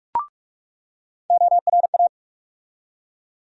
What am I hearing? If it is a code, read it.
Morse code, heard as a keyed tone, about 47 wpm, 700 Hz. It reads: ORA
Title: USA Call Sign Postfixes - 3 Letters 45wpm